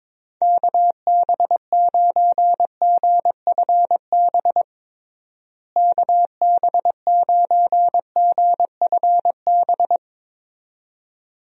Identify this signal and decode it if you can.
Morse code, heard as a keyed tone, about 22 wpm, 700 Hz. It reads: KB9GFB KB9GFB
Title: USA Call Signs - 6 Characters - Repeated 2x - 22wpm